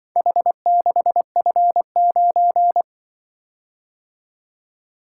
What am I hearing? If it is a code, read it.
Morse code, heard as a keyed tone, about 24 wpm, 700 Hz. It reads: H6F9
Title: Speed-Racing - Four Letter-Number Combinations - 20wpm